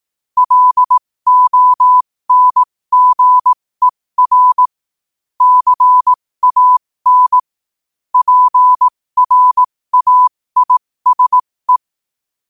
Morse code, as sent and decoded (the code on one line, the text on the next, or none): .-.. --- -. --. .-. / -.-. .- -. / .--. .-. .- .. ... .
LONGER CAN PRAISE